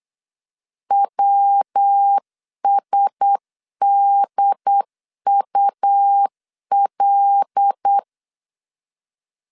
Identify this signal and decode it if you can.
Morse code, heard as a keyed tone, about 8 wpm, 800 Hz. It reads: WSDUL
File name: CW / DET / 1 6 1500C S C 8wpm 5219